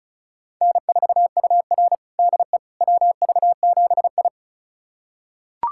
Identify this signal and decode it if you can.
Morse code, heard as a keyed tone, about 36 wpm, 700 Hz. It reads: N4UR DE WV7I
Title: USA Callsign DE Callsign - 4 Characters - 35wpm